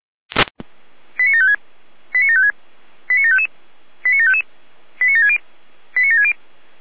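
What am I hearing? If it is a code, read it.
Five-tone selective call (ZVEI), heard as quick runs of five tones, tones of about 70 ms. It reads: 89756 89756 89755 89755 89760 89760
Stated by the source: ZVEI five-tone call received by SDR.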